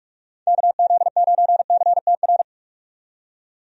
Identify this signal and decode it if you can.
Morse code, heard as a keyed tone, about 46 wpm, 700 Hz. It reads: KZ9CTR
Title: USA Call Signs - 6 Characters - 45wpm